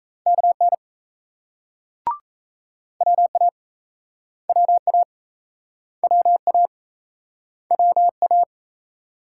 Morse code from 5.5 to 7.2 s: WA

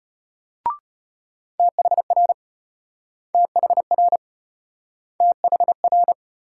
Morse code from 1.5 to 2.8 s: THR

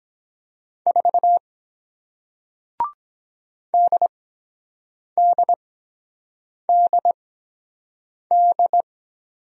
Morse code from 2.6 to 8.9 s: DDDD

